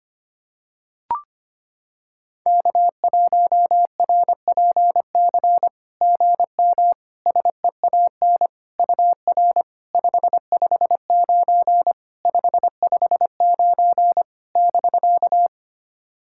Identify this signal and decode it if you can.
Morse code, heard as a keyed tone, about 25 wpm, 690 Hz. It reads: K1RPC GM SEAN UR 559 559 <BK>
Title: SOTA QSO Practice 25wpm